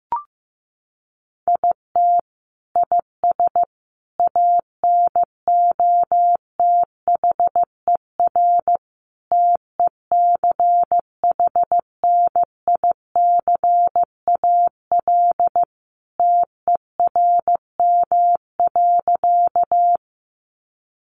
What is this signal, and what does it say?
Morse code, heard as a keyed tone, about 15 wpm, 700 Hz. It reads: IT IS ANOTHER TECHNICAL TERM.
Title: Sentences from Top 1100 Words 15wpm